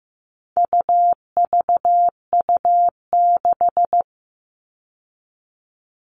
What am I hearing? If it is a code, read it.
Morse code, heard as a keyed tone, about 15 wpm, 700 Hz. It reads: UVU6
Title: All Cut Numbers - 4-Digit 15wpm